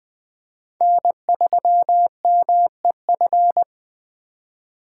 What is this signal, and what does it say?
Morse code, heard as a keyed tone, about 20 wpm, 700 Hz. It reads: N3MEF
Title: USA Call Signs - 5 Characters - 4x Character Spacing - 20wpm